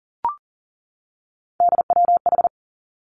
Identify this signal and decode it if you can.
Morse code, heard as a keyed tone, about 38 wpm, 700 Hz. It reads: DWH
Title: Three Challenging Letter-Number Combinations 40wpm